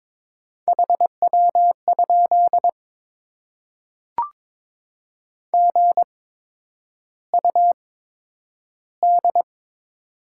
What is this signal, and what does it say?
Morse code, heard as a keyed tone, about 22 wpm, spaced as for 9 wpm, 700 Hz. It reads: HW? GUD
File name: Top 20 QSO Elements - 8x Character Spacing - 22wpm